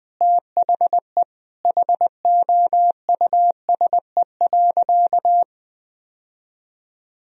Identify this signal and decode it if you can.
Morse code, heard as a keyed tone, about 20 wpm, 700 Hz. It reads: THE HOUSE.
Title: Sentences from Top 1000 Words - Review Part 4 - 20wpm